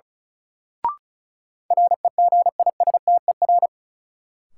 Morse code, heard as a keyed tone, about 36 wpm, 700 Hz. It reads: REGISTER
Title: Sentences from Top 1200 Words - New Word Drill Alt - 35wpm